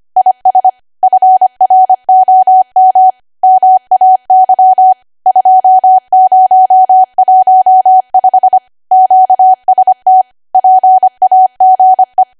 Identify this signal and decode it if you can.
Morse code, heard as a keyed tone, about 25 wpm, 750 Hz. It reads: IS FROM MAY 2015 QST PAGE
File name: Archive / 25 WPM / 25WPM